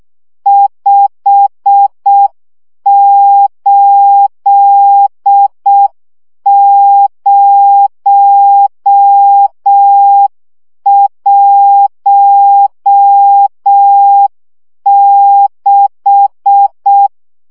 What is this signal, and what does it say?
Morse code, heard as a keyed tone, about 6 wpm, 800 Hz. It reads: 58016
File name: morze vizsga 2011-1 5